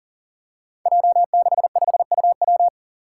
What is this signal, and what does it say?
Morse code, heard as a keyed tone, about 39 wpm, 700 Hz. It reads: J65UW
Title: Five Challenging Letter-Number Combinations 40wpm